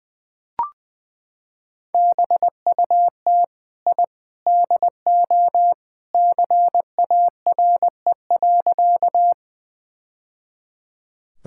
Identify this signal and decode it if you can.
Morse code, heard as a keyed tone, about 20 wpm, 700 Hz. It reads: BUT I DO CARE.
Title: Sentences with 3-Word Phrases - 20wpm